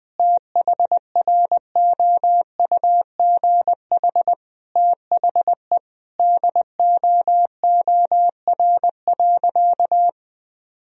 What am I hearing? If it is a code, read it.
Morse code, heard as a keyed tone, about 21 wpm, 700 Hz. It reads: THROUGH THE DOOR.
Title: Sentences from Top 400 Words and Abbreviations 20wpm